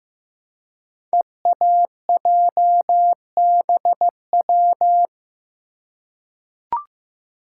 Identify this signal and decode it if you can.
Morse code, heard as a keyed tone, about 15 wpm, 700 Hz. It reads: EAJBW
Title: Code Groups - 5 Letters - 15wpm